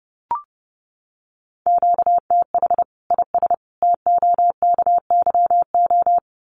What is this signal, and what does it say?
Morse code, heard as a keyed tone, about 30 wpm, 700 Hz. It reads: QTH IS TOKYO
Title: Common QSO Phrases 30wpm